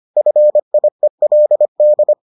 Morse code, heard as a keyed tone, about 25 wpm, 600 Hz. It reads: FIELD